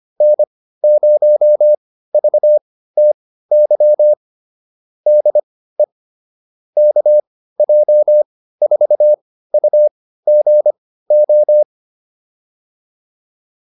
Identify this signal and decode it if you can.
Morse code, heard as a keyed tone, about 25 wpm, spaced as for 15 wpm, 600 Hz. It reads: N0VTY DE KJ4UGO